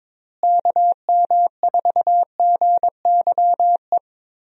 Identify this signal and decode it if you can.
Morse code, heard as a keyed tone, about 22 wpm, 700 Hz. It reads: KM4GYE
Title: USA Call Signs - 6 Characters - Repeated 3x - 22wpm